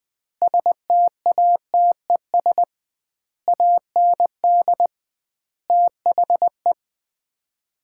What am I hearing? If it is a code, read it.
Morse code, heard as a keyed tone, about 20 wpm, 700 Hz. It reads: STATES AND THE
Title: 2x Word-Spacing - Sets of 5 Words - 20wpm